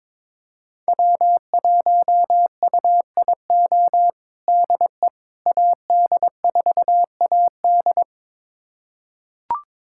Morse code, heard as a keyed tone, about 22 wpm, 700 Hz. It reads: W1UIO DE AD4AD
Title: USA Callsign DE Callsign - 5 Characters - 22wpm